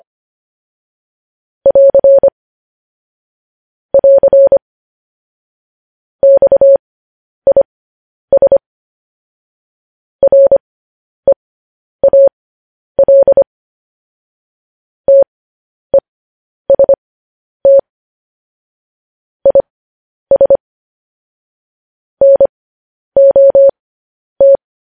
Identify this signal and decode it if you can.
Morse code, heard as a keyed tone, about 25 wpm, spaced as for 10 wpm, 550 Hz. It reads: <AR> <AR> XIS REAL TEST IS NOT
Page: news headlines in morse code at 10 wpm